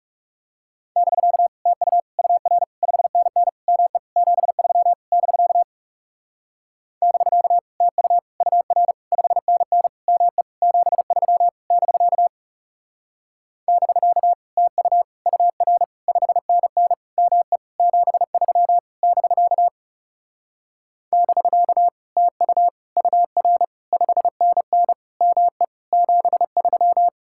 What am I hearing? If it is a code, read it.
Morse code, heard as a keyed tone, about 37 wpm, 700 Hz. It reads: <BK> TU UR 5NN ME 73 <BK> <BK> TU UR 5NN ME 73 <BK> <BK> TU UR 5NN ME 73 <BK> <BK> TU UR 5NN ME 73
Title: Speed-Racing Common QSO Phrases - 30wpm